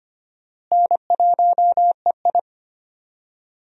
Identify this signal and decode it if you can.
Morse code, heard as a keyed tone, about 25 wpm, 700 Hz. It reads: N1EI
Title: USA Call Signs - 4 Characters - Repeated 2x - 25wpm